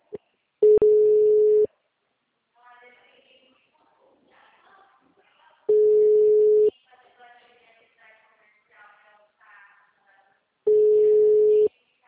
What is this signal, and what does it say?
Ringback tone (Europe, 425 Hz), with speech.